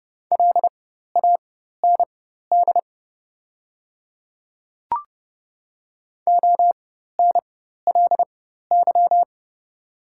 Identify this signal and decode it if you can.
Morse code, heard as a keyed tone, about 30 wpm, spaced as for 14 wpm, 700 Hz. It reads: LAND ONLY
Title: 4x Character Spacing - Top 300 Words - 30wpm